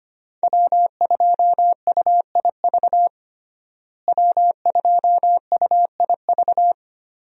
Morse code, heard as a keyed tone, about 25 wpm, 700 Hz. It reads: W2UIV W2UIV
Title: USA Call Signs - 5 Characters - Repeated 3x - 25wpm